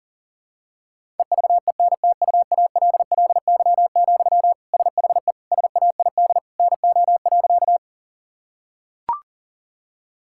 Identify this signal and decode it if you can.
Morse code, heard as a keyed tone, about 40 wpm, 700 Hz. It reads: EVENTUALLY, SHE SAID NO.
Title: Sentences from Top 1000 Words - Review Part 6 - 40wpm